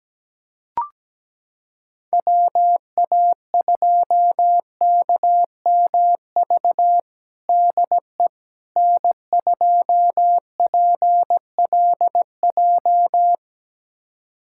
Morse code, heard as a keyed tone, about 17 wpm, 700 Hz. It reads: WA2KMV DE N2PLJ
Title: USA Callsign DE Callsign 17wpm